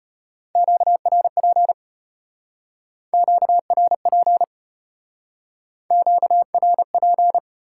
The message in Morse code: QRP QRP QRP

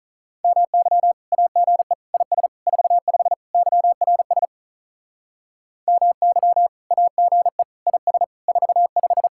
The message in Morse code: MY AGE IS 45 YRS MY AGE IS 45

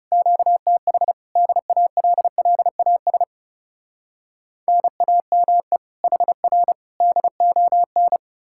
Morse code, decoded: QTH DALLAS NAME HR DON

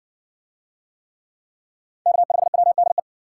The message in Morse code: K5WDE

